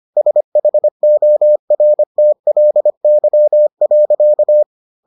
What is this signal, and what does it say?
Morse code, heard as a keyed tone, about 25 wpm, 600 Hz. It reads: SHORTLY.